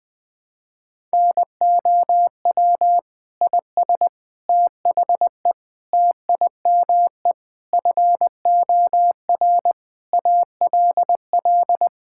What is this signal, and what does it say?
Morse code, heard as a keyed tone, about 20 wpm, 700 Hz. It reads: NOW IS THE TIME FOR ALL
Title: Pangram Challenge v2 - 20wpm